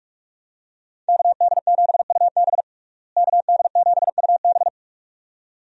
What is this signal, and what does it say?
Morse code, heard as a keyed tone, about 47 wpm, 700 Hz. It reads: KD7UB KD7UB
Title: USA Call Signs - 5 Characters - Repeated 2x - 45wpm